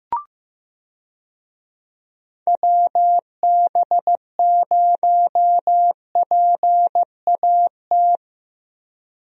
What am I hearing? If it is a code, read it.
Morse code, heard as a keyed tone, about 15 wpm, 700 Hz. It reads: WB0PAT
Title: USA Call Signs - 6 Characters - 15wpm